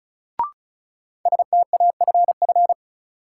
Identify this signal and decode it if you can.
Morse code, heard as a keyed tone, about 34 wpm, 700 Hz. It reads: STAFF
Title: Sentences from Top 1200 Words - New Word Drill Alt - 35wpm